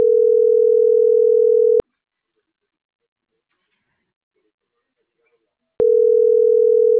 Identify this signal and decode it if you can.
Ringback tone (North America), 440 + 480 Hz.
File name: CENTRAL SOTOVA / out roberto.miranda